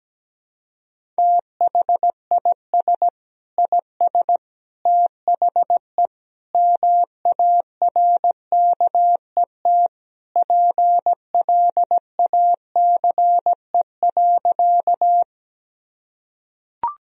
Morse code, heard as a keyed tone, about 17 wpm, 700 Hz. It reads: THIS IS THE MARKET PLACE.
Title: Sentences from Top 1000 Words - Review Part 3 - 17wpm